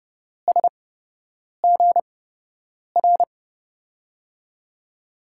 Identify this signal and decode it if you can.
Morse code, heard as a keyed tone, about 30 wpm, spaced as for 8 wpm, 700 Hz. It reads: SGR
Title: USA Call Signs - 6 Characters - 8x Character Spacing - 30wpm